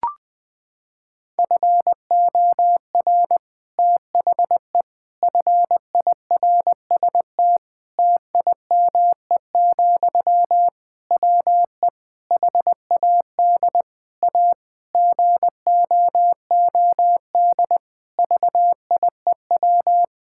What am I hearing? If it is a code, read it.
Morse code, heard as a keyed tone, about 20 wpm, 700 Hz. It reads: FOR THE FIRST TIME, WE HAD A GOOD VIEW
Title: Sentences from Top 1000 Words - Review Part 6 - 20wpm